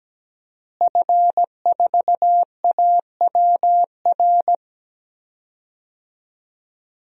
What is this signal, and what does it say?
Morse code, heard as a keyed tone, about 17 wpm, 700 Hz. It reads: F4AWR
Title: DXCC-100 Call Signs - 5 Characters - Repeated 2x - 17wpm